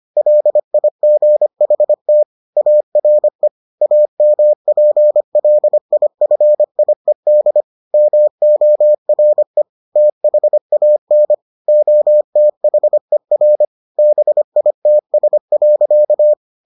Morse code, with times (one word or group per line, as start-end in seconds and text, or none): 0.2-2.2 LIGHT
2.6-3.5 ARE
3.8-7.6 AMPLIFIED
7.9-9.6 MORE
10.0-11.3 THAN
11.7-13.6 OTHER
14.0-16.3 BITS.